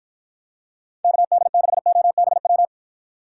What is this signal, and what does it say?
Morse code, heard as a keyed tone, about 53 wpm, 700 Hz. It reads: KD6OBW